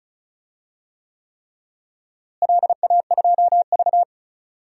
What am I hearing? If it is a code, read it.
Morse code, heard as a keyed tone, about 36 wpm, 700 Hz. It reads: LA2V
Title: DXCC-100 Call Signs - 4 Characters - Repeated 3x - 35wpm